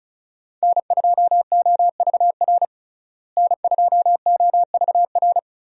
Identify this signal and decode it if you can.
Morse code, heard as a keyed tone, about 34 wpm, 700 Hz. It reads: N2OVR N2OVR
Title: USA Call Signs - 5 Characters - Repeated 3x - 35wpm